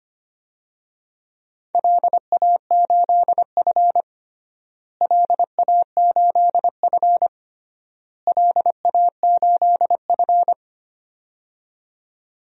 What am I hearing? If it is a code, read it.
Morse code, heard as a keyed tone, about 25 wpm, 700 Hz. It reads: LA8F LA8F LA8F
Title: DXCC-100 Call Signs - 4 Characters - Repeated 3x - 25wpm